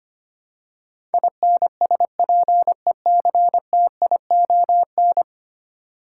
Morse code, heard as a keyed tone, about 25 wpm, 700 Hz. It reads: INSPECTION
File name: Ten Letter Words - 25wpm